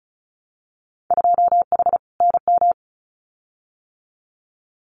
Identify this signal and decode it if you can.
Morse code, heard as a keyed tone, about 36 wpm, 700 Hz. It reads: 2H NM
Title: Winter Field Day Exchanges Encore 35wpm